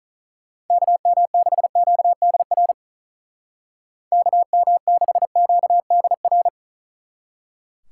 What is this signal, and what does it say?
Morse code, heard as a keyed tone, about 43 wpm, 700 Hz. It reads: KM6QDR KM6QDR
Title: Speed Racing - USA 2x3 Call Signs 35wpm